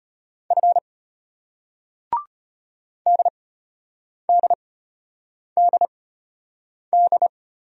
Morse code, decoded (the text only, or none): F DDDD